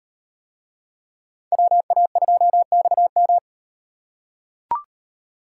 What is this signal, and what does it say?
Morse code, heard as a keyed tone, about 35 wpm, 700 Hz. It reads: WA2XM